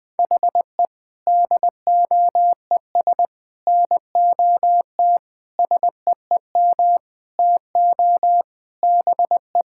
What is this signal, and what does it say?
Morse code, heard as a keyed tone, about 21 wpm, 700 Hz. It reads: HE DOES NOT SEEM TO BE
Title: Sentences with 5-Word Phrases - 20wpm